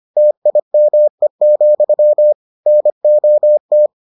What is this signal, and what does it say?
Morse code, heard as a keyed tone, about 25 wpm, 600 Hz. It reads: TIME, NOT